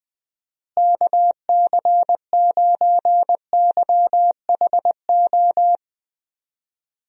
Morse code, heard as a keyed tone, about 20 wpm, 700 Hz. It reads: KC9YHO
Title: USA Call Signs - 6 Characters - Repeated 2x - 20wpm